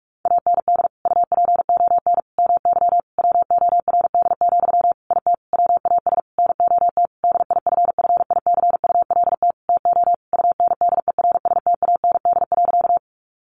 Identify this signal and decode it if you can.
Morse code, heard as a keyed tone, about 44 wpm, 700 Hz. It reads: AND UPON MY WORD, IT WAS NOT DIFFICULT TO UNDERSTAND.
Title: Sentences from Top 1100 Words 45wpm